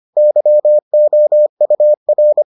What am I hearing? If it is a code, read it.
Morse code, heard as a keyed tone, about 25 wpm, 600 Hz. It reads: YOUR